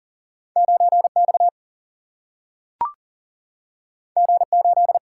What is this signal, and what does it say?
Morse code, heard as a keyed tone, about 40 wpm, 700 Hz. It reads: 9X G8
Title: Single-Letter Single-Number 40wpm